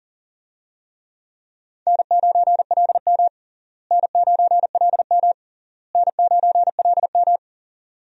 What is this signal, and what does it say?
Morse code, heard as a keyed tone, about 40 wpm, 700 Hz. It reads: N9LM N9LM N9LM
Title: USA Call Signs - 4 Characters - Repeated 3x - 40wpm